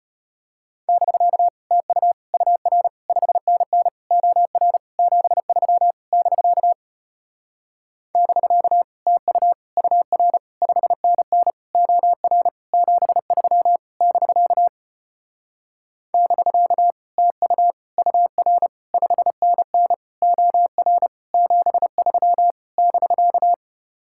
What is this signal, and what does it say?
Morse code, heard as a keyed tone, about 33 wpm, 700 Hz. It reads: <BK> TU UR 5NN OR 73 <BK> <BK> TU UR 5NN OR 73 <BK> <BK> TU UR 5NN OR 73 <BK>